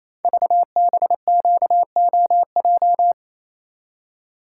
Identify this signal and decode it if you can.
Morse code, heard as a keyed tone, about 29 wpm, 700 Hz. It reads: VBQOJ
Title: Code Groups - 5 Letters - 28wpm